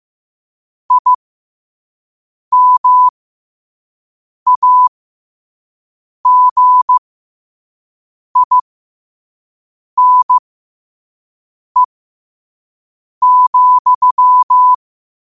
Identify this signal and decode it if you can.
Morse code, heard as a keyed tone, about 15 wpm, spaced as for 5 wpm, 1000 Hz. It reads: IMAGINE,
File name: 5wpm